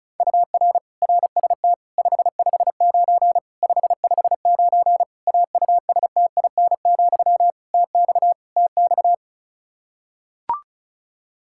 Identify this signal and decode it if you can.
Morse code, heard as a keyed tone, about 36 wpm, 700 Hz. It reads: UR RST 559 559 AUSTIN, TX TX
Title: K1USN SST QSO Practice 35wpm